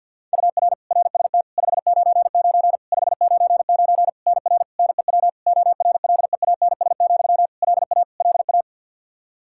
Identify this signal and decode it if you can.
Morse code, heard as a keyed tone, about 50 wpm, 700 Hz. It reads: UR RST 599 599 NR NEW ORLEANS, LA LA
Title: Practice QSO - Exchange 1 - Beginning - 50wpm